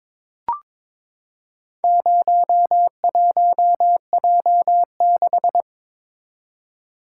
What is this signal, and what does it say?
Morse code, heard as a keyed tone, about 22 wpm, 700 Hz. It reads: 01J6
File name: practice / 22wpm / Four Letter-Number Combinations - 22wpm